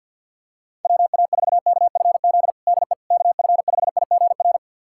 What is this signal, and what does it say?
Morse code, heard as a keyed tone, about 49 wpm, 700 Hz. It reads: WA4KFZ DE KF5IGR
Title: USA Callsign DE Callsign - 6 Characters - 50wpm